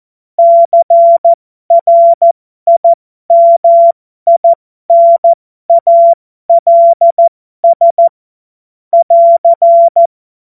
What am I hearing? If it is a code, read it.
Morse code, heard as a keyed tone, about 14 wpm, spaced as for 12 wpm, 680 Hz. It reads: CRIMINALS <AR>
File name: Morsekurs / Witze